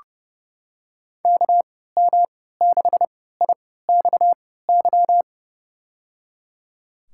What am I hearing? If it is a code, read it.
Morse code, heard as a keyed tone, about 30 wpm, spaced as for 17 wpm, 700 Hz. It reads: KM6IXY
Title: USA Call Signs - 6 Characters - 3x Character Spacing - 30wpm